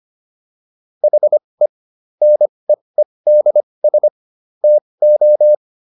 Morse code, heard as a keyed tone, about 25 wpm, spaced as for 20 wpm, 600 Hz. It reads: HE NEEDS TO